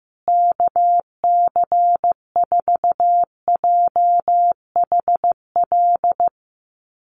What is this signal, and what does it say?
Morse code, heard as a keyed tone, about 14 wpm, 700 Hz. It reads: KC4JHL
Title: USA Call Signs Repeated 2x 15wpm